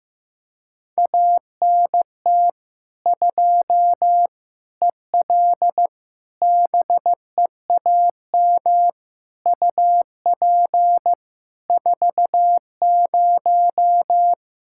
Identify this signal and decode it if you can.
Morse code, heard as a keyed tone, about 15 wpm, 700 Hz. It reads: ANT 2 EL BEAM UP 40
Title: Practice QSO - Exchange 2 - Info - 15wpm